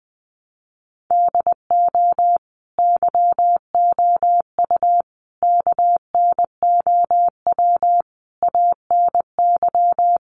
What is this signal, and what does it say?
Morse code, heard as a keyed tone, about 20 wpm, 700 Hz. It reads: DO YOU KNOW ANY
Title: Sentences from Top 1200 Words 20wpm